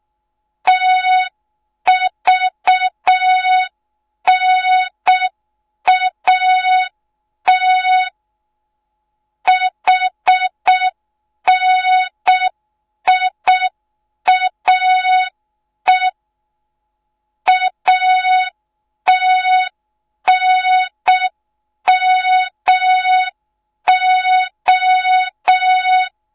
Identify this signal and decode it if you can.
Morse code, heard as a keyed tone, about 6 wpm, 3700 Hz. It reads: TVNAT HNIAE ATNMO